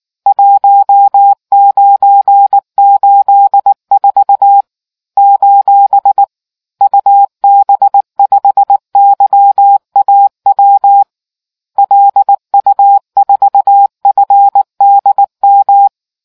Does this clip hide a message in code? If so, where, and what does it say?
Yes: Morse code from 0.3 to 15.9 s, heard as a keyed tone, about 19 wpm, 800 Hz. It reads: 1984 : UB5YAW LU4FDM